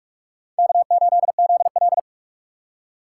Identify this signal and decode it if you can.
Morse code, heard as a keyed tone, about 38 wpm, 700 Hz. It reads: K8ZL